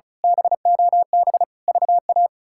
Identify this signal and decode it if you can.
Morse code, heard as a keyed tone, about 34 wpm, 700 Hz. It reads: BOB VA
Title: K1USN SST QSO Practice - 35wpm